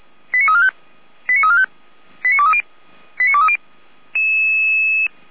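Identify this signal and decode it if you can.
Five-tone selective call (ZVEI), heard as quick runs of five tones, tones of about 70 ms. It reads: 89345 89345 89230 89230
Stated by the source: ZVEI five-tone selective call received on an SDR, logged by Feuerwehr Orlamünde (fire brigade).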